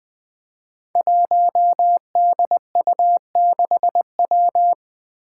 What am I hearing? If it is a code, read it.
Morse code, heard as a keyed tone, about 20 wpm, 700 Hz. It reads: 1DU6W
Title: Five Challenging Letter-Number Combinations 20wpm